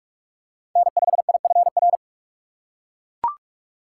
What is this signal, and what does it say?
Morse code, heard as a keyed tone, about 42 wpm, 700 Hz. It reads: N5IFR
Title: USA Call Signs - 6x Character Spacing - 45wpm